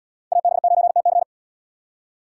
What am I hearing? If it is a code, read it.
Morse code, heard as a keyed tone, about 68 wpm, 700 Hz. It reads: AB9TC